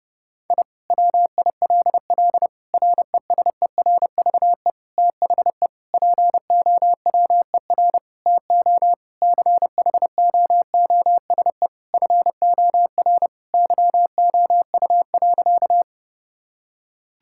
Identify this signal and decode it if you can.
Morse code, heard as a keyed tone, about 30 wpm, 700 Hz. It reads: I WILL RESERVE THE POWER TO CHOOSE FOR YOU.
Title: Sentences from Top 1000 Words - Review Part 6 - 30wpm